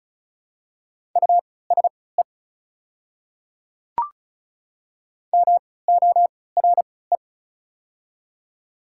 Morse code, heard as a keyed tone, about 36 wpm, spaced as for 20 wpm, 690 Hz. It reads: USE MORE
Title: 3x Character Spacing - Top 500 Words - 35wpm